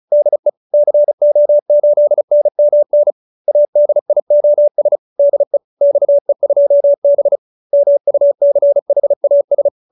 Morse code, heard as a keyed tone, about 34 wpm, 590 Hz. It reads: DE CO8NMN ADIOS DE XE2B MUCHAS
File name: QSO304 35